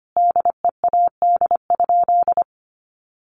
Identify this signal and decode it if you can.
Morse code, heard as a keyed tone, about 25 wpm, 700 Hz. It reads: DEAD?
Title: Sentences from Top 1500 Words 25wpm